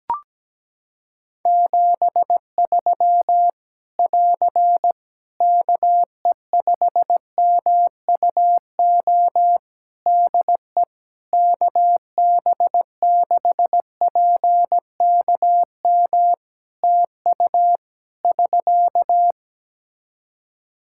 Morse code, heard as a keyed tone, about 17 wpm, 700 Hz. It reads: 73 <AR> KE5MUO DE KB6PKM TU <SK>
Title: Practice QSO - Exchange 4 - Ending - 17wpm